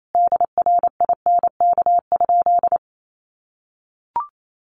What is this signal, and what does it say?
Morse code, heard as a keyed tone, about 28 wpm, 700 Hz. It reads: DRINK?
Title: Sentences from Top 1500 Words 28wpm